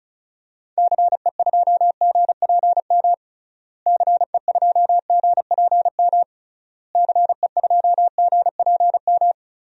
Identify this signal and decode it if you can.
Morse code, heard as a keyed tone, about 34 wpm, 700 Hz. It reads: CE2GPM CE2GPM CE2GPM
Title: DXCC-100 Call Signs - 6 Characters - Repeated 3x - 35wpm